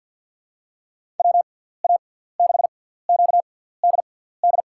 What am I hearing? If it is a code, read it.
Morse code, heard as a keyed tone, about 48 wpm, spaced as for 17 wpm, 700 Hz. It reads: WA6QDD
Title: USA Call Signs - 6 Characters - 6x Character Spacing - 50wpm